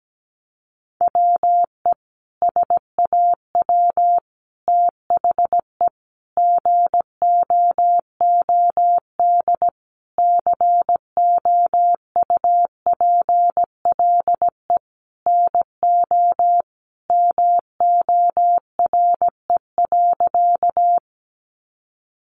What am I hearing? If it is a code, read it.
Morse code, heard as a keyed tone, about 17 wpm, 700 Hz. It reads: WE SAW THE GOOD COUPLE NO MORE.
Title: Sentences from Top 1100 Words 17wpm